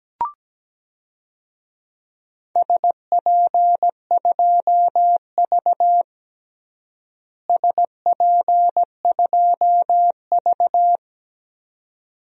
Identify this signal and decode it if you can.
Morse code, heard as a keyed tone, about 17 wpm, 700 Hz. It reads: SP2V SP2V
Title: DXCC-100 Call Signs - 4 Characters - Repeated 2x - 17wpm